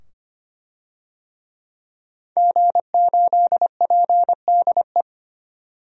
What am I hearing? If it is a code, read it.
Morse code, heard as a keyed tone, about 25 wpm, 700 Hz. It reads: G8PDE
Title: DXCC-100 Call Signs - 5 Characters - 25wpm